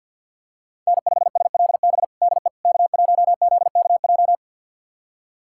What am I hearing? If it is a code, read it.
Morse code, heard as a keyed tone, about 52 wpm, 700 Hz. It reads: N5SLB DE K1ZKJ
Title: USA Callsign DE Callsign 50wpm